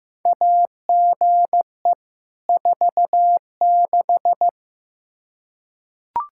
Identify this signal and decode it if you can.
Morse code, heard as a keyed tone, about 15 wpm, 700 Hz. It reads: AGE 46
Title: K1USN SST QSO Practice 15wpm